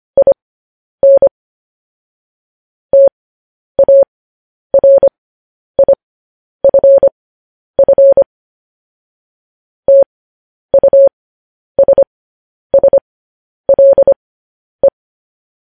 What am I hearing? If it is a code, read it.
Morse code, heard as a keyed tone, about 25 wpm, spaced as for 10 wpm, 550 Hz. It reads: IN TARIFF TUSSLE
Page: news headlines in morse code at 10 wpm